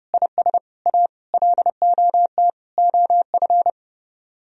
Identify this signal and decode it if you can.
Morse code, heard as a keyed tone, about 31 wpm, 700 Hz. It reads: IS A LOT OF